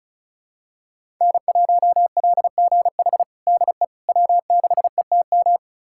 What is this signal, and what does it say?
Morse code, heard as a keyed tone, about 34 wpm, 700 Hz. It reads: N1LGH DE W6ETM
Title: USA Callsign DE Callsign - 5 Characters - 35wpm